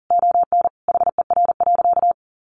Morse code, heard as a keyed tone, about 40 wpm, 700 Hz. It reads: ON HER.
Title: Sentences from Top 1300 Words 40wpm